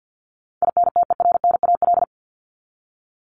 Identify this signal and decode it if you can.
Morse code, heard as a keyed tone, about 50 wpm, 700 Hz. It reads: INTERNAL
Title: Top 1251-1500 Words 50wpm